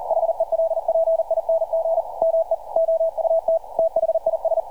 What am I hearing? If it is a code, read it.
Morse code, heard as a keyed tone, about 39 wpm, 660 Hz, with radio noise. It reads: HELPING ME OUT THIS
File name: archived / CRSnet